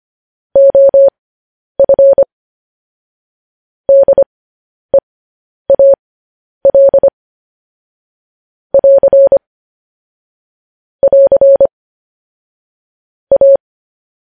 Morse code, heard as a keyed tone, about 25 wpm, spaced as for 10 wpm, 550 Hz. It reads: OF DEAL <AR> <AR> A